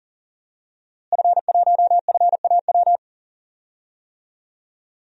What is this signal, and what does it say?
Morse code, heard as a keyed tone, about 40 wpm, 700 Hz. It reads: F1FAW
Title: DXCC-100 Call Signs - 5 Characters - Repeated 3x - 40wpm